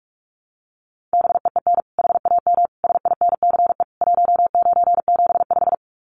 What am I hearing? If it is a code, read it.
Morse code, heard as a keyed tone, about 44 wpm, 700 Hz. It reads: BEEN HAM SINCE 1975
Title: Common QSO Phrases 45wpm